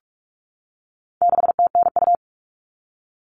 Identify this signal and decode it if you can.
Morse code, heard as a keyed tone, about 44 wpm, 700 Hz. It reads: 6TNU
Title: All Cut Numbers - 4-Digit 45wpm